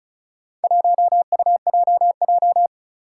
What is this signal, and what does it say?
Morse code, heard as a keyed tone, about 36 wpm, 700 Hz. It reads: 1UJJ
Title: Four Challenging Letter-Number Combinations 35wpm